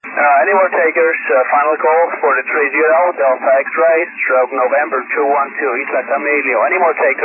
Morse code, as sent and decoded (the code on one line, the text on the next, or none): none